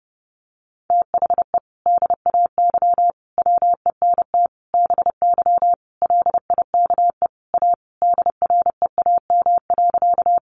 Morse code, heard as a keyed tone, about 30 wpm, 700 Hz. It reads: THE DAY WENT BY LIKE A DREAM.
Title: Sentences from Top 1500 Words 30wpm